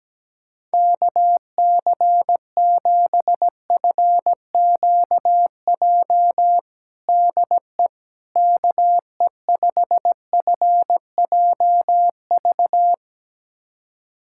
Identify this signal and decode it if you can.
Morse code, heard as a keyed tone, about 17 wpm, 700 Hz. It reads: KC7FQJ DE KE5FJV
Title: USA Callsign DE Callsign - 6 Characters - 17wpm